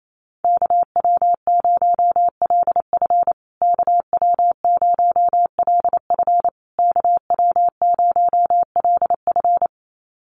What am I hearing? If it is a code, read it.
Morse code, heard as a keyed tone, about 28 wpm, 700 Hz. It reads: KW0LF KW0LF KW0LF
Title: USA Call Signs Repeated 3x 28wpm